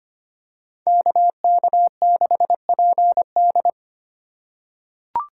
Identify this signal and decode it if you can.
Morse code, heard as a keyed tone, about 25 wpm, 700 Hz. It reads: KK6PD
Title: USA Call Signs - 4x Character Spacing - 25wpm